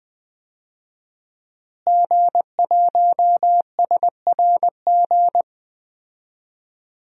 Morse code, heard as a keyed tone, about 20 wpm, 700 Hz. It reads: G1SRG
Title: DXCC-100 Call Signs - 5 Characters - Repeated 2x - 20wpm